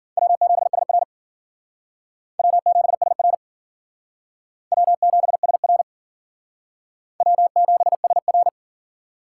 Morse code, heard as a keyed tone, about 52 wpm, 700 Hz. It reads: W7SR W7SR W7SR W7SR